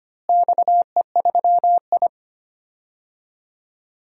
Morse code, heard as a keyed tone, about 25 wpm, 700 Hz. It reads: XE3I
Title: DXCC-100 Call Signs - 4 Characters - Repeated 2x - 25wpm